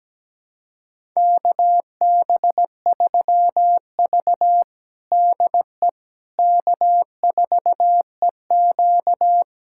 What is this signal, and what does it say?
Morse code, heard as a keyed tone, about 17 wpm, 700 Hz. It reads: KB3V DE K4EQ
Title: USA Callsign DE Callsign - 4 Characters - 17wpm